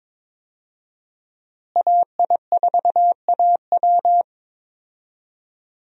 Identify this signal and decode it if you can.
Morse code, heard as a keyed tone, about 22 wpm, 700 Hz. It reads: AI4AW
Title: USA Call Signs - 5 Characters - Repeated 3x - 22wpm